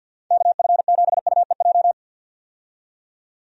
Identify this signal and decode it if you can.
Morse code, heard as a keyed tone, about 51 wpm, 700 Hz. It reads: KF7UEJ
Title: USA Call Signs - 6 Characters - 50wpm